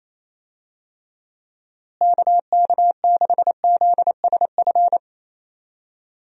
Morse code, heard as a keyed tone, about 28 wpm, 700 Hz. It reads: KK6ZSF
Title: USA Call Signs - 6 Characters - Repeated 2x - 28wpm